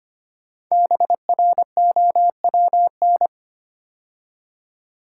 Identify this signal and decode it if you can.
Morse code, heard as a keyed tone, about 25 wpm, 700 Hz. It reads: BROWN
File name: 5 Letter Words - First 3 Letters Given - 25wpm